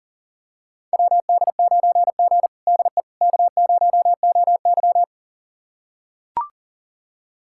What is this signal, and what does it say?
Morse code, heard as a keyed tone, about 40 wpm, 700 Hz. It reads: WD9G DE K0OY
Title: USA Callsign DE Callsign - 4 Characters - 40wpm